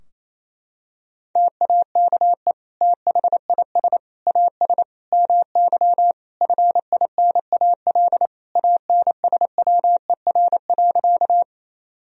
Morse code, heard as a keyed tone, about 28 wpm, 700 Hz. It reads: TAKE THIS AS MY FINAL ANSWER.
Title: Sentences from Top 1000 Words - Review Part 2 - 28wpm